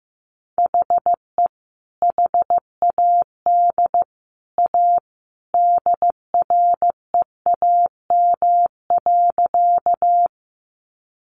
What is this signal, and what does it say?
Morse code, heard as a keyed tone, about 15 wpm, 700 Hz. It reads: HE HAD A DREAM.
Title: Sentences from Top 1500 Words 15wpm